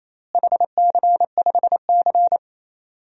Morse code, heard as a keyed tone, about 28 wpm, 700 Hz. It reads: HC5C
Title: DXCC-100 Call Signs - 4 Characters - Repeated 3x - 28wpm